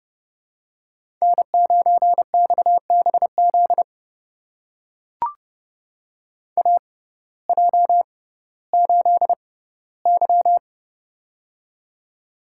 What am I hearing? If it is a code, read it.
Morse code, heard as a keyed tone, about 30 wpm, spaced as for 17 wpm, 700 Hz. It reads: N9XBZ AJ8Y
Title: USA Call Signs - 6x Character Spacing - 30wpm